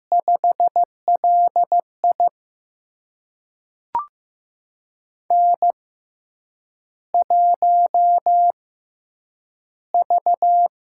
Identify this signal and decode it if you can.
Morse code, heard as a keyed tone, about 15 wpm, spaced as for 8 wpm, 700 Hz. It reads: 5LI N1V
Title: USA Call Signs - 4 Characters - 6x Character Spacing - 15wpm